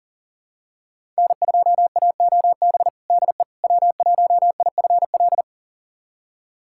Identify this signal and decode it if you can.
Morse code, heard as a keyed tone, about 40 wpm, 700 Hz. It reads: N2AOB DE W1IFL
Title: USA Callsign DE Callsign - 5 Characters - 40wpm